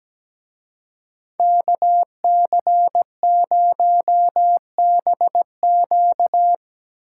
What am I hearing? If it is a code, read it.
Morse code, heard as a keyed tone, about 20 wpm, 700 Hz. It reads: KC0BQ
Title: Speed Racing - USA 2x2 Call Signs 17wpm